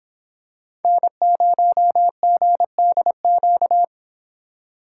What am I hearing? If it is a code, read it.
Morse code, heard as a keyed tone, about 25 wpm, 700 Hz. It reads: N0GDQ